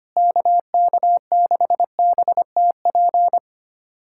Morse code, heard as a keyed tone, about 25 wpm, 700 Hz. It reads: KK6BTP
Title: USA Callsign DE Callsign 25wpm